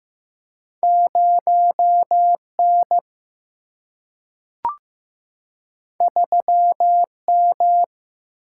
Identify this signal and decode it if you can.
Morse code, heard as a keyed tone, about 15 wpm, 700 Hz. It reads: 0N 3M
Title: Single-Letter Single-Number 15wpm